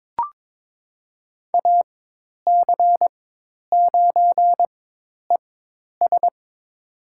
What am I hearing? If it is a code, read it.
Morse code, heard as a keyed tone, about 22 wpm, spaced as for 10 wpm, 700 Hz. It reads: AC9ES
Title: USA Call Signs - 5 Characters - 4x Character Spacing - 22wpm